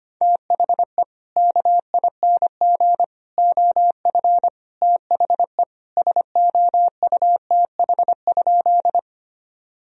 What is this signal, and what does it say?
Morse code, heard as a keyed tone, about 25 wpm, 700 Hz. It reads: THE KING OF THE SOUTH?
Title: Sentences from Top 1000 Words - Review Part 2 - 25wpm